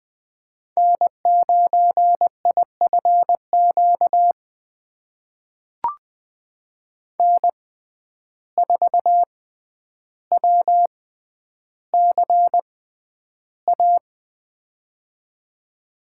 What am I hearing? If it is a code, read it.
Morse code, heard as a keyed tone, about 20 wpm, spaced as for 10 wpm, 700 Hz. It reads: N9IFQ N4WCA
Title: USA Call Signs - 5 Characters - 6x Character Spacing - 20wpm